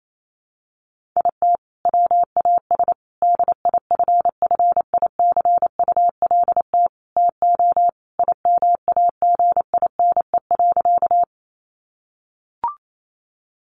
Morse code, heard as a keyed tone, about 28 wpm, 700 Hz. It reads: IT WAS DIFFICULT TO IMAGINE.
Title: Sentences from Top 1400 Words 28wpm